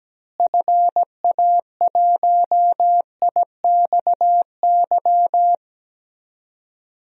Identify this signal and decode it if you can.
Morse code, heard as a keyed tone, about 17 wpm, 700 Hz. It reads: FA1IXY
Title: DXCC-100 Call Signs - 6 Characters - Repeated 2x - 17wpm